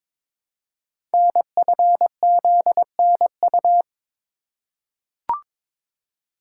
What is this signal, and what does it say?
Morse code, heard as a keyed tone, about 22 wpm, 700 Hz. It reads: NFZNU
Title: Code Groups - 5 Letters - 22wpm